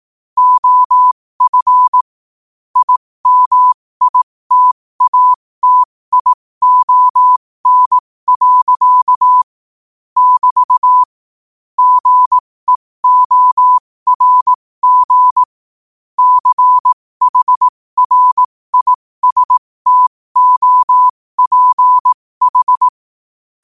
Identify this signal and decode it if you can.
Morse code, heard as a keyed tone, about 18 wpm, spaced as for 15 wpm, 1000 Hz. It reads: OF IMITATION. <BT> GEORG CHRISTOPH
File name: qotdfiles / 15wpm